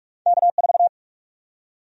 Morse code, heard as a keyed tone, about 46 wpm, 700 Hz. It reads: K4
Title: USA Call Sign Prefixes - 45wpm